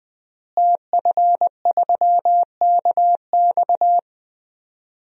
Morse code, heard as a keyed tone, about 20 wpm, 700 Hz. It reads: TF3KX
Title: DXCC-100 Call Signs - 5 Characters - Repeated 2x - 20wpm